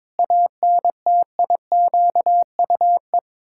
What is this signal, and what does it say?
Morse code, heard as a keyed tone, about 22 wpm, 700 Hz. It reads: ANTIQUE
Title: Pangram Challenge v2 - 22wpm